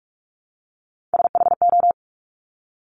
Morse code, heard as a keyed tone, about 46 wpm, 700 Hz. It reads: SHO
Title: Top Three Letter Combinations in Words 45wpm